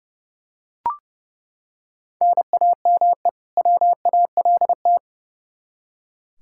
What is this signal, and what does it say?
Morse code, heard as a keyed tone, about 30 wpm, 700 Hz. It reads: NAME WALT
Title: Practice QSO - Exchange 1 - Beginning - 30wpm